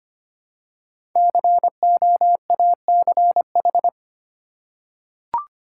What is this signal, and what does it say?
Morse code, heard as a keyed tone, about 25 wpm, 700 Hz. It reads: COACH
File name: 5 Letter Words - First 3 Letters Given - 25wpm